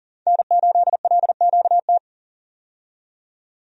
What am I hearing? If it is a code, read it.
Morse code, heard as a keyed tone, about 39 wpm, 700 Hz. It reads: N8LQT
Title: USA Call Signs - 5 Characters - 40wpm